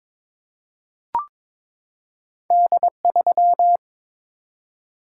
Morse code, heard as a keyed tone, about 23 wpm, 700 Hz. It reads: D3
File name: practice / 22wpm / Two Letter-Number Combinations - 22wpm